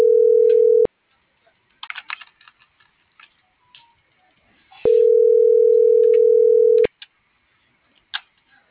Ringback tone (North America), 440 + 480 Hz.